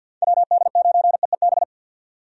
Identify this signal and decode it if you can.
Morse code, heard as a keyed tone, about 50 wpm, 700 Hz. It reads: WD9EEB
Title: USA Call Signs - 6 Characters - 50wpm